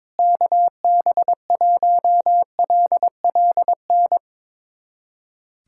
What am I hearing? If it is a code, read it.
Morse code, heard as a keyed tone, about 22 wpm, 700 Hz. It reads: KB1LLN
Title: USA Callsign DE Callsign - 6 Characters - 22wpm